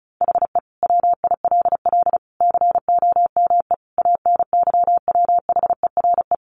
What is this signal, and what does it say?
Morse code, heard as a keyed tone, about 34 wpm, 700 Hz. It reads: HE WILL COME ANYWHERE
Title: Sentences from Top 1400 Words 35wpm